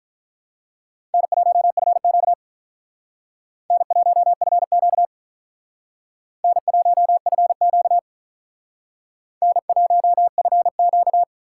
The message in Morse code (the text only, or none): N1FQ N1FQ N1FQ N1FQ